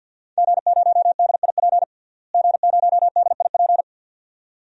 Morse code, heard as a keyed tone, about 52 wpm, 700 Hz. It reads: G0DIP G0DIP
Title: DXCC-100 Call Signs - 5 Characters - Repeated 2x - 50wpm